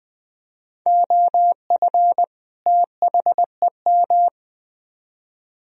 Morse code, heard as a keyed tone, about 23 wpm, 700 Hz. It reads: OF THEM
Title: Speed-Racing - Sets of 2 Words 20wpm